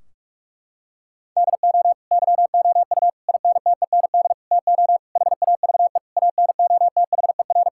Morse code, heard as a keyed tone, about 46 wpm, 700 Hz. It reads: DO YOU INTEND TO HAVE ANOTHER
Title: Sentences from Top 1600 Words 45wpm